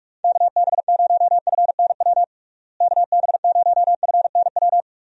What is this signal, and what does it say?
Morse code, heard as a keyed tone, about 46 wpm, 700 Hz. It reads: KB0FNW KB0FNW
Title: USA Call Signs - 6 Characters - Repeated 2x - 45wpm